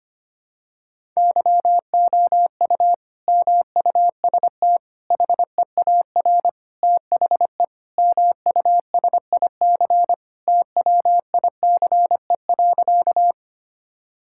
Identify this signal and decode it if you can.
Morse code, heard as a keyed tone, about 25 wpm, 700 Hz. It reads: YOU MUST HEAR THE MUSIC TWICE.
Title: Sentences from Top 1600 Words 25wpm